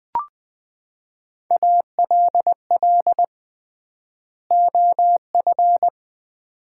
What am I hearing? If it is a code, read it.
Morse code, heard as a keyed tone, about 20 wpm, 700 Hz. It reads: ALL OF
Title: 3x Word-Spacing - Sets of 2 Words - 20wpm